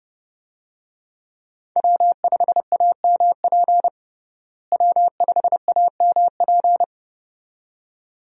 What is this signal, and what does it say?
Morse code, heard as a keyed tone, about 30 wpm, 700 Hz. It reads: W5AMP W5AMP
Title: USA Call Signs - 5 Characters - Repeated 2x - 30wpm